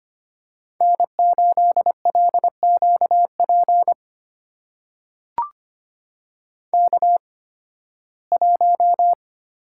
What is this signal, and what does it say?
Morse code, heard as a keyed tone, about 25 wpm, spaced as for 16 wpm, 700 Hz. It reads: N8LQP K1
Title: USA Call Signs - 5 Characters - 8x Character Spacing - 25wpm